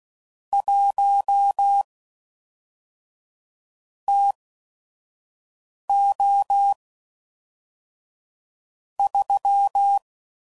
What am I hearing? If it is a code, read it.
Morse code, heard as a keyed tone, about 16 wpm, spaced as for 5 wpm, 800 Hz. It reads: 1 TO 3
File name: CW Code Groups @5WPM / Weather Sets / Weather 5 5